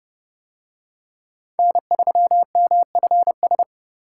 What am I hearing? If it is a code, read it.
Morse code, heard as a keyed tone, about 26 wpm, 700 Hz. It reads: N3MFS